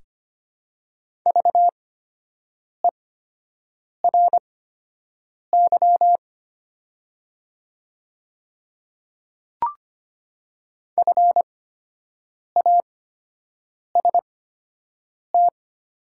Morse code, heard as a keyed tone, about 25 wpm, spaced as for 7 wpm, 690 Hz. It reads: VERY FAST
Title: 8x Character Spacing - Top 500 Words - 25wpm